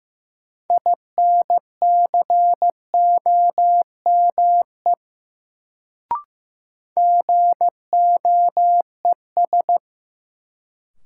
Morse code, heard as a keyed tone, about 15 wpm, 700 Hz. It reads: INCOME GOES
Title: Sentences from Top 1000 Words - Review Part 5 - New Word Drill - 15wpm